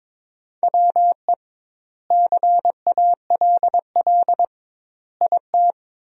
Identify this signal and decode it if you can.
Morse code, heard as a keyed tone, about 21 wpm, 700 Hz. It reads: WE CALL IT
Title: 2x Word-Spacing - Sentences from Top 400 Words - 22wpm